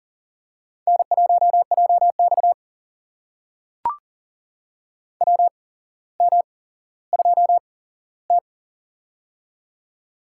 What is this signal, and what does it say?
Morse code, heard as a keyed tone, about 40 wpm, spaced as for 17 wpm, 700 Hz. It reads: N1JX WM2T
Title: USA Call Signs - 4 Characters - 8x Character Spacing - 40wpm